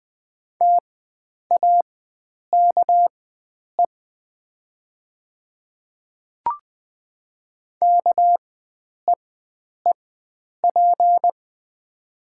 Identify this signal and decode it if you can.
Morse code, heard as a keyed tone, about 20 wpm, spaced as for 9 wpm, 690 Hz. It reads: TAKE KEEP
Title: 4x Character Spacing - Top 500 Words - 20wpm